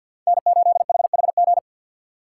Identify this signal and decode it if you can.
Morse code, heard as a keyed tone, about 52 wpm, 690 Hz. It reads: N8HHG
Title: K1USN SST QSO Practice - 50wpm